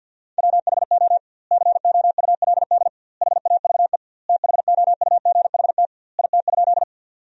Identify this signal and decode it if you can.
Morse code, heard as a keyed tone, about 53 wpm, 700 Hz. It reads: WHO COULD HAVE THOUGHT IT?
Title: 2x Word-Spacing - Sentences from Top 300 Words - 50wpm